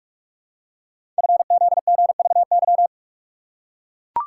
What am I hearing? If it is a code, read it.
Morse code, heard as a keyed tone, about 44 wpm, 700 Hz. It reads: FZGVY
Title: Code Groups - 5 Letters - 45wpm